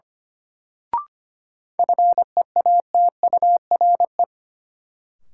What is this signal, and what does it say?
Morse code, heard as a keyed tone, about 25 wpm, 700 Hz. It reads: FEATURE
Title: Sentences from Top 800 Words - New Word Drill - 25wpm